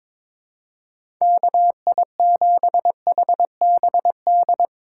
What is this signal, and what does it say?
Morse code, heard as a keyed tone, about 21 wpm, 700 Hz. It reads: KI7HBD
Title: USA Call Signs - 2x Character Spacing - 22wpm